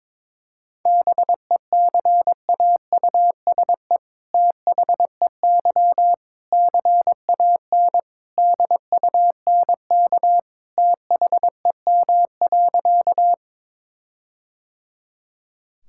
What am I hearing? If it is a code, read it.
Morse code, heard as a keyed tone, about 22 wpm, 700 Hz. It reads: BECAUSE THEY CAN DUNK THEM.
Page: morse-code-ninja-practice-22wpm